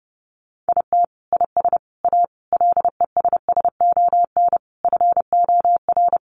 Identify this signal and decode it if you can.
Morse code, heard as a keyed tone, about 30 wpm, 700 Hz. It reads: IT IS A LESSON FOR
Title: Sentences from Top 1200 Words 30wpm